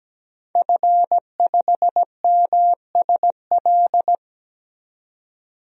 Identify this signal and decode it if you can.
Morse code, heard as a keyed tone, about 17 wpm, 700 Hz. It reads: F5MSL
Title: DXCC-100 Call Signs - 5 Characters - 17wpm